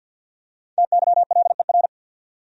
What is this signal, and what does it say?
Morse code, heard as a keyed tone, about 54 wpm, 700 Hz. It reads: TYLER
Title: USA Common Names - 50wpm